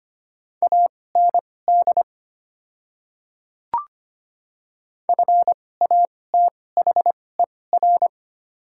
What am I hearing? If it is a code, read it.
Morse code, heard as a keyed tone, about 25 wpm, spaced as for 18 wpm, 700 Hz. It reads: AND FATHER
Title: 2x Character Spacing - Top 500 Words - 25wpm